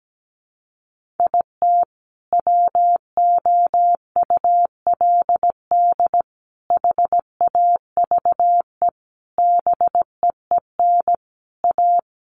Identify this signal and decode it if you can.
Morse code, heard as a keyed tone, about 17 wpm, 700 Hz. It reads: IT WOULD HAVE BEEN A
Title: Sets of 5 Words Encore 17wpm